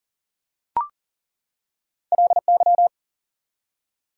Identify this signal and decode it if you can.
Morse code, heard as a keyed tone, about 40 wpm, 700 Hz. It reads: LY
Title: Prefixes and Suffixes 40wpm